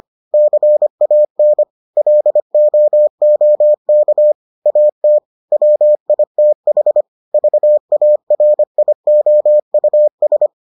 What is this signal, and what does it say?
Morse code, heard as a keyed tone, about 25 wpm, 600 Hz. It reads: CAN LOOK AT WITH VARIOUS